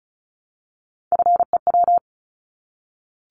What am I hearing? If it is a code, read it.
Morse code, heard as a keyed tone, about 36 wpm, 700 Hz. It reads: FEW
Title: Multi-letter Phonemes Repeated 3x 35wpm